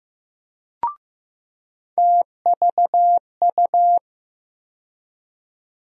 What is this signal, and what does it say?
Morse code, heard as a keyed tone, about 15 wpm, 700 Hz. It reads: TVU
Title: USA Call Sign Postfixes - 3 Letters 15wpm